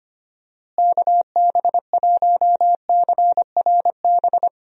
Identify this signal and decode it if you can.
Morse code, heard as a keyed tone, about 25 wpm, 700 Hz. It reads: KB1CRB
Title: USA Call Signs - 6 Characters - Repeated 2x - 25wpm